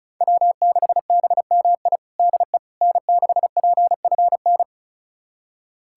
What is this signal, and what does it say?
Morse code, heard as a keyed tone, about 36 wpm, 700 Hz. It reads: W6BMI DE N6PFN
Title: USA Callsign DE Callsign - 5 Characters - 35wpm